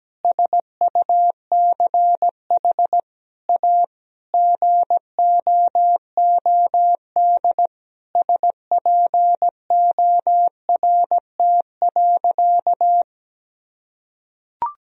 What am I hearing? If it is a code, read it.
Morse code, heard as a keyed tone, about 17 wpm, 700 Hz. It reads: SUCH A GOOD SPORT.
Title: Sentences from Top 1000 Words - Review Part 4 - 17wpm